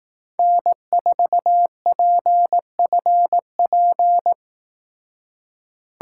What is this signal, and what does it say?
Morse code, heard as a keyed tone, about 18 wpm, 700 Hz. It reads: N4PFP
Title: Speed Racing - USA Call Signs 15wpm